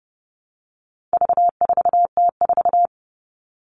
Morse code, heard as a keyed tone, about 30 wpm, 700 Hz. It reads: V4T4
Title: All Cut Numbers - 4-Digit 30wpm